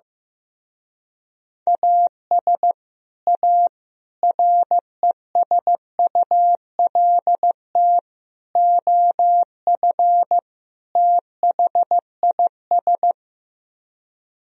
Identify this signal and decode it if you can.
Morse code, heard as a keyed tone, about 15 wpm, 700 Hz. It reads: AS A RESULT OF THIS
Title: Speed-Racing - Sets of 5 Words 15wpm